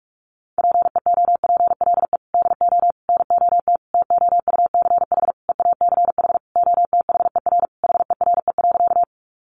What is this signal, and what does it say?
Morse code, heard as a keyed tone, about 44 wpm, 700 Hz. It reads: PEOPLE DO NOT TOUCH EACH OTHER HERE.
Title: Sentences from Top 1200 Words 45wpm